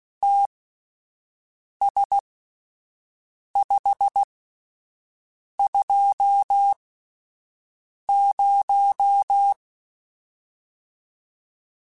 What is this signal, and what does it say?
Morse code, heard as a keyed tone, about 16 wpm, spaced as for 5 wpm, 800 Hz. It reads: TS520
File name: QSOs / QSOs-mp3s / WD0FBT 1@10